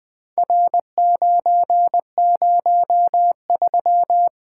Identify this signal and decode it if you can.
Morse code, heard as a keyed tone, about 20 wpm, 700 Hz. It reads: R903
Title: Four Letter-Number Combinations - 20wpm